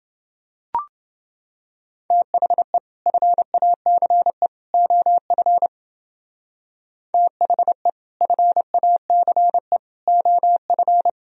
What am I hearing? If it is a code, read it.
Morse code, heard as a keyed tone, about 29 wpm, 700 Hz. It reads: THE FACE OF THE FACE OF